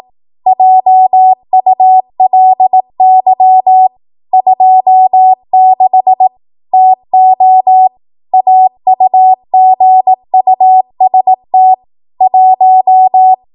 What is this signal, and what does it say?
Morse code, heard as a keyed tone, about 18 wpm, 750 Hz. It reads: JULY 26 TO AUGUST 1
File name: Morse / 18WPM